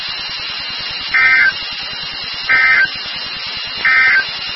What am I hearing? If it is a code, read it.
Emergency Alert System (SAME) tones, heard as harsh buzzing bursts. It is heard from 1.0 to 4.3 s.
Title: NOAA Weather Radio WXK36